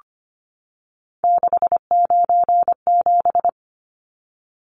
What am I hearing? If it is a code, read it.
Morse code, heard as a keyed tone, about 25 wpm, 700 Hz. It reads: 697
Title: Numbers - 3-Digit 25wpm